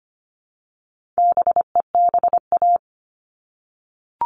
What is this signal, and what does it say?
Morse code, heard as a keyed tone, about 25 wpm, 700 Hz. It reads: BEBA
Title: All Cut Numbers - 4-Digit 25wpm